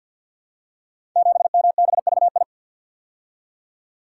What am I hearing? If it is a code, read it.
Morse code, heard as a keyed tone, about 46 wpm, 700 Hz. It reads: 7MBVI
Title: Code Groups - 5 Letter:Numbers - 50wpm